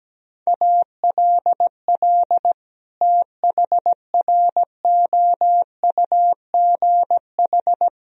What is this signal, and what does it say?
Morse code, heard as a keyed tone, about 17 wpm, 700 Hz. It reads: ALL THROUGH
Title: Sentences from Top 1000 Words - Review Part 2 - 17wpm